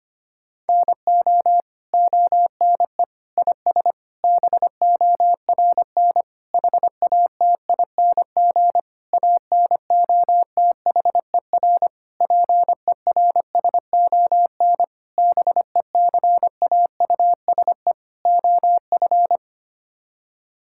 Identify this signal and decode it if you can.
Morse code, heard as a keyed tone, about 25 wpm, 700 Hz. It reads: NO ONE IS BORN HATING ANOTHER PERSON BECAUSE OF
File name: Sets of 9 Words - 25wpm